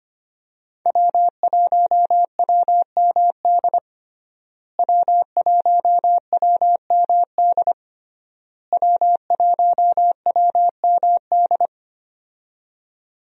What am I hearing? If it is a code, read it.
Morse code, heard as a keyed tone, about 25 wpm, 700 Hz. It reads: W1WMD W1WMD W1WMD
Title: USA Call Signs - 5 Characters - Repeated 3x - 25wpm